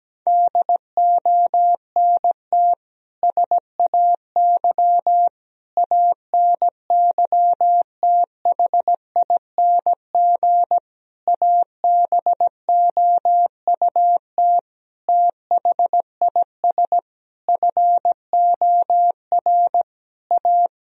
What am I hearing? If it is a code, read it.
Morse code, heard as a keyed tone, about 17 wpm, 700 Hz. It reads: DONT SAY ANYTHING ABOUT THIS FOR A
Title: Sentences from Top 1000 Words - Review Part 1 - 17wpm